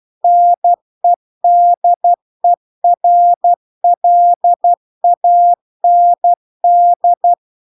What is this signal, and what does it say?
Morse code, heard as a keyed tone, about 12 wpm, 700 Hz. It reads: NEDERLAND
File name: cw-cursus-VERON-a03